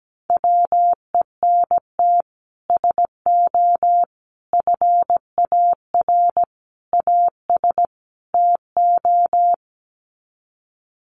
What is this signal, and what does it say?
Morse code, heard as a keyed tone, about 17 wpm, 700 Hz. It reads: WENT SO FAR AS TO
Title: Sets of 5 Words Encore 17wpm